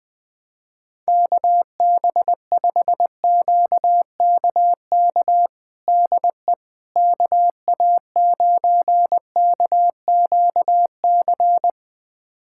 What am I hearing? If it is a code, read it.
Morse code, heard as a keyed tone, about 20 wpm, 700 Hz. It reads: KB5QKK DE KA9KQC
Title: USA Callsign DE Callsign - 6 Characters - 20wpm